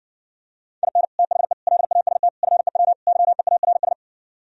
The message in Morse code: IN THE LAST FEW YEARS

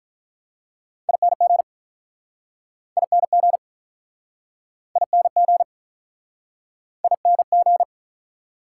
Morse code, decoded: ING ING ING ING

